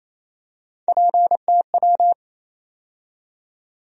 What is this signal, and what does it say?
Morse code, heard as a keyed tone, about 28 wpm, 700 Hz. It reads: PTW
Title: USA Call Sign Postfixes - 3 Letters 28wpm